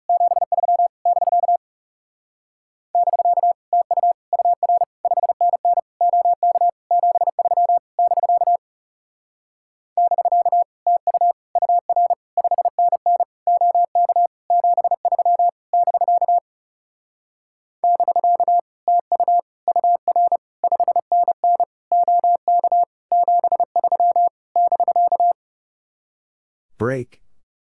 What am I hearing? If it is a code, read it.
Morse code, heard as a keyed tone, about 35 wpm, 700 Hz. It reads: 73 <BK> <BK> TU UR 5NN OK 73 <BK> <BK> TU UR 5NN OK 73 <BK> <BK> TU UR 5NN OK 73 <BK>
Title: Speed-Racing Common QSO Phrases - 30wpm